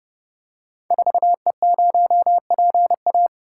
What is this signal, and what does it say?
Morse code, heard as a keyed tone, about 30 wpm, 700 Hz. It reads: 4E0PA
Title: Code Groups - 5 Letter:Numbers - 30wpm